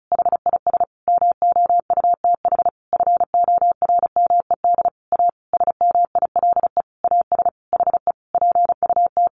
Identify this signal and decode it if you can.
Morse code, heard as a keyed tone, about 34 wpm, 700 Hz. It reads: HIS MOUTH FORMED A SMILE AS HE PUT
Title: A Sound Of Thunder 35wpm